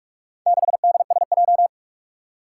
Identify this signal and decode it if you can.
Morse code, heard as a keyed tone, about 46 wpm, 700 Hz. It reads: 6DSJ
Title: Four Challenging Letter-Number Combinations 45wpm